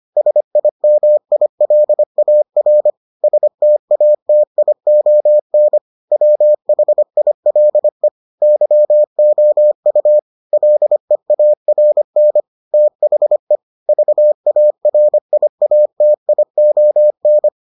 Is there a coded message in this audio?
Yes: Morse code from 0.2 to 17.5 s, heard as a keyed tone, about 25 wpm, 600 Hz. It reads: SIMILAR STATION WHILE YOU LEARN THE VARIATION